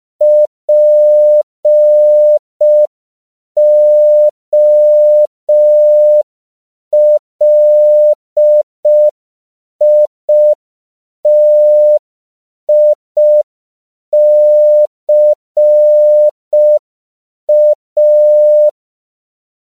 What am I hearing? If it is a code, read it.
Morse code, heard as a keyed tone, about 5 wpm, 600 Hz. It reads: POLITICA
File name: CW / 5PPM